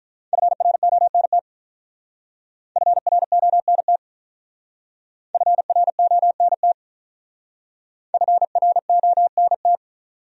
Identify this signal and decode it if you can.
Morse code, heard as a keyed tone, about 45 wpm, 700 Hz. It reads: FRONT FRONT FRONT FRONT